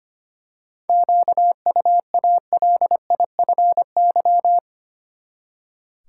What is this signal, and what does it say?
Morse code, heard as a keyed tone, about 25 wpm, 700 Hz. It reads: QUALIFY
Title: Seven Letter Words - 25wpm